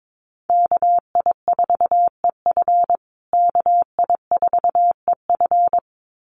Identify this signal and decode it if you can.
Morse code, heard as a keyed tone, about 22 wpm, 700 Hz. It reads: KI4EF KI4EF
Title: USA Call Signs Repeated 2x 22wpm